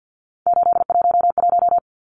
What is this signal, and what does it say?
Morse code, heard as a keyed tone, about 50 wpm, 700 Hz. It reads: 811
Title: Numbers - 3-Digit 50wpm